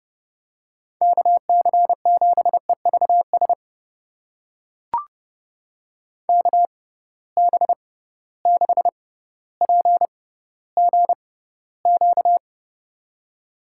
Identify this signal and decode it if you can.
Morse code, heard as a keyed tone, about 30 wpm, spaced as for 15 wpm, 700 Hz. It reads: KC7EVS KB6PGQ
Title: USA Call Signs - 6 Characters - 6x Character Spacing - 30wpm